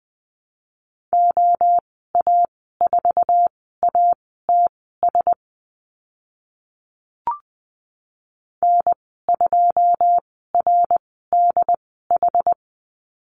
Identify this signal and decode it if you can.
Morse code, heard as a keyed tone, about 20 wpm, spaced as for 14 wpm, 700 Hz. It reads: OA4ATS N2RDH
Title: 082 Lesson 14g - Number 2 - Call Signs 20wpm